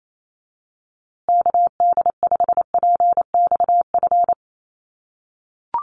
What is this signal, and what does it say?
Morse code, heard as a keyed tone, about 28 wpm, 700 Hz. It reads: KD5PXF
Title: USA Call Signs Repeated 3x 28wpm